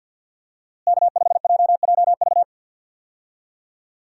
Morse code, heard as a keyed tone, about 46 wpm, 700 Hz. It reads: K5JJV
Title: USA Call Signs - 5 Characters - Repeated 2x - 50wpm